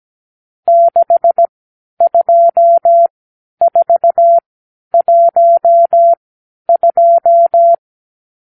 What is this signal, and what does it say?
Morse code, heard as a keyed tone, about 17 wpm, spaced as for 10 wpm, 680 Hz. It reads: 62412
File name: k7qo / Lesson 54 (9)